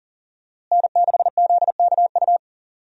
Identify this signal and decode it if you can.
Morse code, heard as a keyed tone, about 40 wpm, 700 Hz. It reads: N6ZKU
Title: USA Call Signs - 5 Characters - Repeated 2x - 40wpm